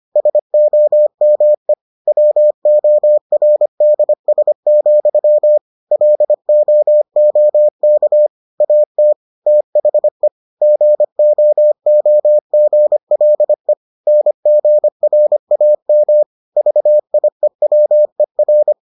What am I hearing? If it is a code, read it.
Morse code, heard as a keyed tone, about 25 wpm, 600 Hz. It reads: SOME WORDS, LOOK AT THE GOOGLE NGRAM VIEWER